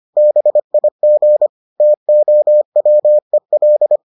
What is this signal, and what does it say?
Morse code, heard as a keyed tone, about 25 wpm, 600 Hz. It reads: BIG TOWEL